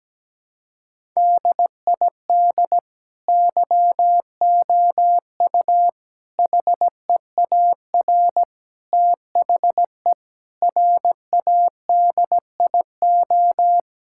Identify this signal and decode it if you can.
Morse code, heard as a keyed tone, about 17 wpm, 700 Hz. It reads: DID YOU HEAR THE RADIO
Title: Sentences from Top 1000 Words - Review Part 5 - 17wpm